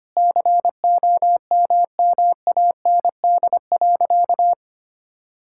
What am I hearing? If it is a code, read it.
Morse code, heard as a keyed tone, about 25 wpm, 700 Hz. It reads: COMMAND.
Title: Sentences from Top 1600 Words 25wpm